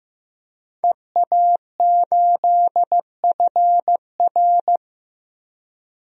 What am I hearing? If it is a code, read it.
Morse code, heard as a keyed tone, about 15 wpm, 700 Hz. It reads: EA8FR
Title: DXCC-100 Call Signs - 5 Characters - Repeated 3x - 15wpm